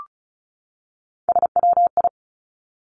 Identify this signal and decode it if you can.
Morse code, heard as a keyed tone, about 36 wpm, 700 Hz. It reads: SWI
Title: Three Challenging Letter-Number Combinations 35wpm